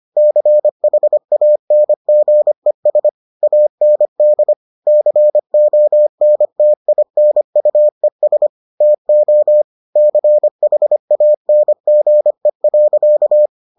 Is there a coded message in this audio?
Yes: Morse code from 0.2 to 13.5 s, heard as a keyed tone, about 25 wpm, 600 Hz. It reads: CHANGES AND CONTINUES TO CHANGE.